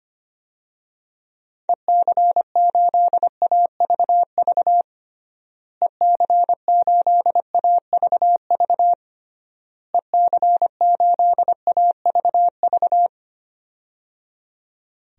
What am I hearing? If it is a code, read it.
Morse code, heard as a keyed tone, about 25 wpm, 700 Hz. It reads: EC8AVV EC8AVV EC8AVV
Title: DXCC-100 Call Signs - 6 Characters - Repeated 3x - 25wpm